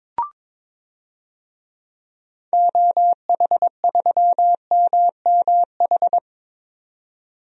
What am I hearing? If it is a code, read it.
Morse code, heard as a keyed tone, about 22 wpm, 700 Hz. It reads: OH3MMH
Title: DXCC-100 Call Signs - 6 Characters - 22wpm